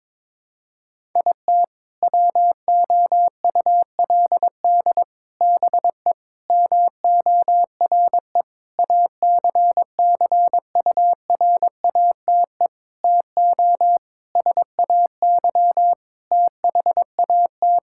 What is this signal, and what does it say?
Morse code, heard as a keyed tone, about 21 wpm, 700 Hz. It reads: IT WOULD BE MORE ACCURATE TO SAY THAT
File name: practice / 22wpm / Sentences with 8-Word Phrases - 22wpm